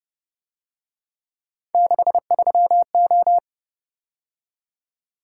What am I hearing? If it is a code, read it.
Morse code, heard as a keyed tone, about 31 wpm, 700 Hz. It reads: 63O
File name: Speed-Racing - Three Letter-Number Combinations - 25wpm